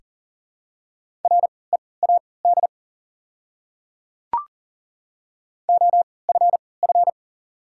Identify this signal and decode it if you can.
Morse code, heard as a keyed tone, about 40 wpm, spaced as for 23 wpm, 690 Hz. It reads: READ OFF
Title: 3x Character Spacing - Top 300 Words - 40wpm